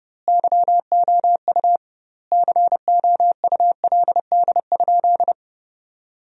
Morse code, heard as a keyed tone, about 30 wpm, 700 Hz. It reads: YOU COULD?
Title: 2x Word-Spacing - Sentences from Top 400 Words - 30wpm